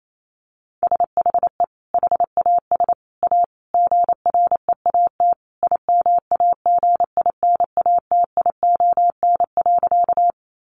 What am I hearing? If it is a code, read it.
Morse code, heard as a keyed tone, about 28 wpm, 700 Hz. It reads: SHE HAS A GREAT IMAGINATION.
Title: Sentences from Top 1300 Words 28wpm